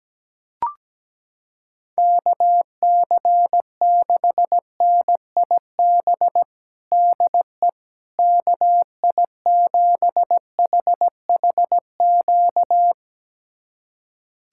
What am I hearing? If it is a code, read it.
Morse code, heard as a keyed tone, about 17 wpm, 700 Hz. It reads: KC6NIB DE KI7HHQ
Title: USA Callsign DE Callsign - 6 Characters - 17wpm